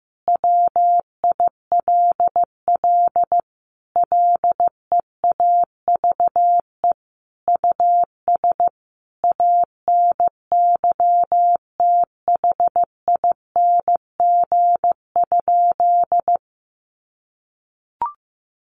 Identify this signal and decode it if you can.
Morse code, heard as a keyed tone, about 15 wpm, 700 Hz. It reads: WILL LEAVE US ANYTHING?
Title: Sentences from Top 1100 Words 15wpm